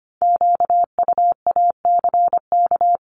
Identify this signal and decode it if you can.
Morse code, heard as a keyed tone, about 25 wpm, 700 Hz. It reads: QUACK
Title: Multi-letter Phonemes Repeated 3x 25wpm